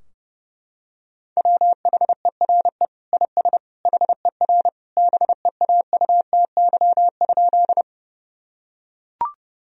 Morse code, heard as a keyed tone, about 32 wpm, 700 Hz. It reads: WHERE IS HER BEAUTY?
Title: Sentences from Top 1000 Words - Review Part 2 - 30wpm